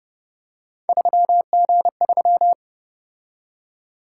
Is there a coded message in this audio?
Yes: Morse code from 0.9 to 2.5 s, heard as a keyed tone, about 29 wpm, 700 Hz. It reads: 3G3